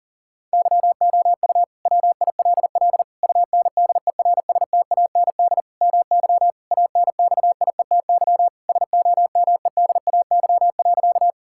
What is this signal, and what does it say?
Morse code, heard as a keyed tone, about 37 wpm, 700 Hz. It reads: YOU WILL UNDERSTAND MY ANXIETY SOMEDAY.
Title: Sentences from Top 1600 Words 40wpm